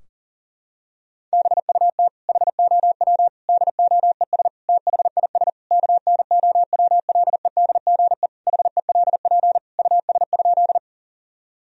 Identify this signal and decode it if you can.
Morse code, heard as a keyed tone, about 40 wpm, 700 Hz. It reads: BUT HOW DOES THIS KNOWLEDGE HELP US?
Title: Sentences from Top 1000 Words - Review Part 3 - 40wpm